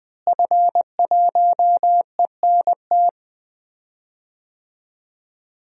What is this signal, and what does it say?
Morse code, heard as a keyed tone, about 20 wpm, 700 Hz. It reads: F1ENT
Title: DXCC-100 Call Signs - 5 Characters - Repeated 2x - 20wpm